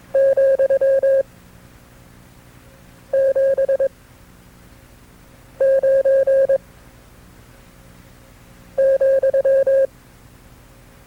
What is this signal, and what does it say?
Morse code, heard as a keyed tone, about 22 wpm, 540 Hz. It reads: , 7 9 ,
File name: FreeFiles / CW-practice / KB6MT-tape2-side2